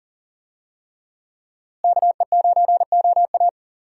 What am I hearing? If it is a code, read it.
Morse code, heard as a keyed tone, about 40 wpm, 700 Hz. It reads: KE9OA
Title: USA Call Signs - 5 Characters - Repeated 3x - 40wpm